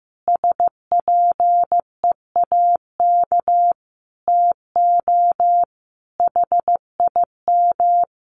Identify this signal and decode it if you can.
Morse code, heard as a keyed tone, about 15 wpm, 700 Hz. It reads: SPEAK TO HIM
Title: Sentences from Top 1500 Words 15wpm